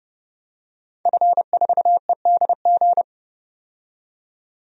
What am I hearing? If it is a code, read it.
Morse code, heard as a keyed tone, about 32 wpm, 700 Hz. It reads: F4EDG
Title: DXCC-100 Call Signs - 5 Characters - 30wpm